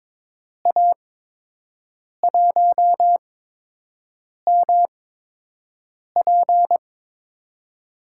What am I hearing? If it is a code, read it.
Morse code, heard as a keyed tone, about 22 wpm, spaced as for 6 wpm, 700 Hz. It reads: A1MP
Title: USA Call Signs - 8x Character Spacing - 22wpm